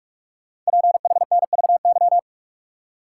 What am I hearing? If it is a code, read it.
Morse code, heard as a keyed tone, about 44 wpm, 700 Hz. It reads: PHNVY
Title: Code Groups - 5 Letters - 45wpm